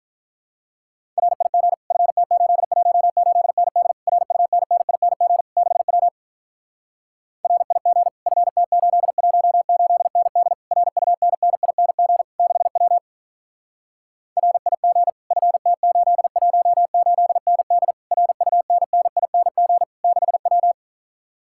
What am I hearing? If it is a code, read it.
Morse code, heard as a keyed tone, about 46 wpm, 700 Hz. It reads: RIG FT818ND RUNNING 6W RIG FT818ND RUNNING 6W RIG FT818ND RUNNING 6W